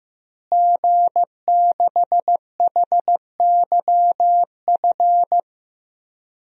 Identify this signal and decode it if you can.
Morse code, heard as a keyed tone, about 15 wpm, 700 Hz. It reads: G6HYF
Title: DXCC-100 Call Signs - 5 Characters - Repeated 3x - 15wpm